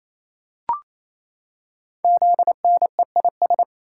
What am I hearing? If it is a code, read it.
Morse code, heard as a keyed tone, about 27 wpm, 700 Hz. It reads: ZNEIS
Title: Code Groups - 5 Letters - 28wpm